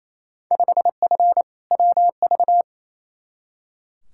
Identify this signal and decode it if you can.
Morse code, heard as a keyed tone, about 28 wpm, 700 Hz. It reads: 5F WV
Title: Field Day Exchanges Encore - 28wpm